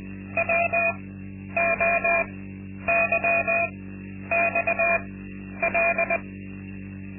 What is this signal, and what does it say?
Morse code, heard as a keyed tone, about 20 wpm, spaced as for 10 wpm, 680 Hz, with noise. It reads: WOYXL
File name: Linksmakalnis-2